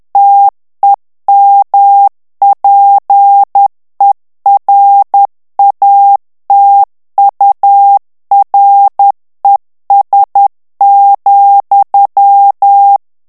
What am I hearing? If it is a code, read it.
Morse code, heard as a keyed tone, about 10 wpm, 800 Hz. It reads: TEMPERATURES,
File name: ANGLAIS texte clair 2 vitesse 10 mots minute